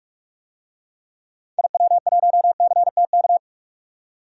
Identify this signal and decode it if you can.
Morse code, heard as a keyed tone, about 46 wpm, 700 Hz. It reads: IW1CTK
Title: DXCC-100 Call Signs - 6 Characters - 45wpm